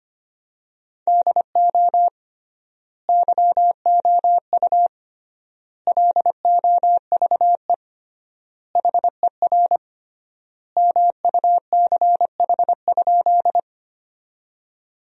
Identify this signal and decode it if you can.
Morse code, heard as a keyed tone, about 25 wpm, 700 Hz. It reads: DO YOU LOVE HER MUCH?
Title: 3x Word-Spacing - Sentences from Top 400 Words - 25wpm